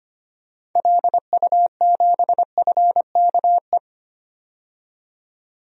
Morse code, heard as a keyed tone, about 25 wpm, 700 Hz. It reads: LU7FKE
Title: DXCC-100 Call Signs - 6 Characters - 25wpm